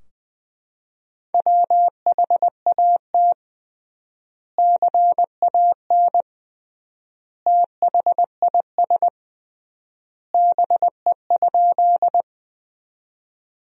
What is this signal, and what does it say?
Morse code, heard as a keyed tone, about 20 wpm, 700 Hz. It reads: WHAT CAN THIS BE?
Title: 3x Word-Spacing - Sentences from Top 100 Words - 20wpm